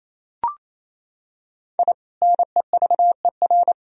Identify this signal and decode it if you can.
Morse code, heard as a keyed tone, about 28 wpm, 700 Hz. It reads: I NEVER
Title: Sentences from Top 1000 Words - Review Part 1 - 28wpm